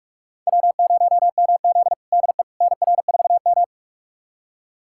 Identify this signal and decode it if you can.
Morse code, heard as a keyed tone, about 43 wpm, 700 Hz. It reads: W0MZ DE NR4M
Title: USA Callsign DE Callsign - 4 Characters - 45wpm